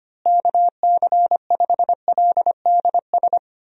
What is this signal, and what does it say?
Morse code, heard as a keyed tone, about 25 wpm, 700 Hz. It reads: KC5LDS
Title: USA Call Signs - 6 Characters - Repeated 3x - 25wpm